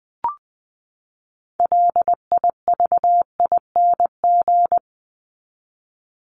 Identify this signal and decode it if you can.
Morse code, heard as a keyed tone, about 20 wpm, 700 Hz. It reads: LIVING
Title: ING Suffix - Top 100 Words with Base Words 20wpm